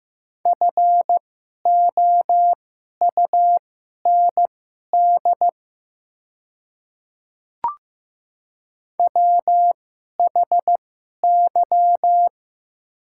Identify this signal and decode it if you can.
Morse code, heard as a keyed tone, about 15 wpm, spaced as for 11 wpm, 700 Hz. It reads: FOUND WHY
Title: 2x Character Spacing - Top 500 Words - 15wpm